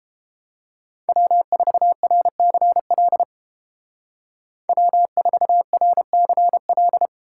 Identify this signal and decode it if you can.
Morse code, heard as a keyed tone, about 30 wpm, 700 Hz. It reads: W4RCL W4RCL